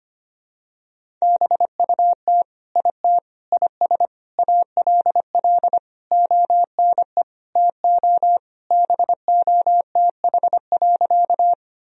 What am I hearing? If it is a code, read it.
Morse code, heard as a keyed tone, about 25 wpm, 700 Hz. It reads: BUT IT IS ALL ONE TO BOTH.
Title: Sentences from Top 1000 Words - Review Part 1 - 25wpm